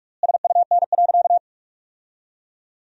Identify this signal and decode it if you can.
Morse code, heard as a keyed tone, about 46 wpm, 700 Hz. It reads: SUN.
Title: 4x Word-Spacing - Sentences from Top 500 Words - 45wpm